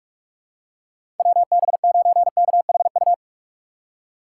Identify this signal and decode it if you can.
Morse code, heard as a keyed tone, about 46 wpm, 700 Hz. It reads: WB9KHU
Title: USA Call Signs - 6 Characters - 4x Character Spacing - 45wpm